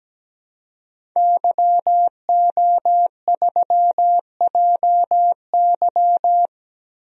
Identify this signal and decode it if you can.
Morse code, heard as a keyed tone, about 17 wpm, 700 Hz. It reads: YO3JY
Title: DXCC-100 Call Signs - 5 Characters - Repeated 3x - 17wpm